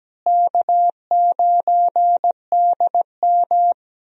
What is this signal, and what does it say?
Morse code, heard as a keyed tone, about 17 wpm, 700 Hz. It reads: K9DM
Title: USA Call Signs - 4 Characters - Repeated 3x - 17wpm